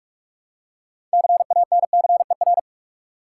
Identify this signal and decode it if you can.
Morse code, heard as a keyed tone, about 46 wpm, 700 Hz. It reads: CANCER
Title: Six Letter Words - 45wpm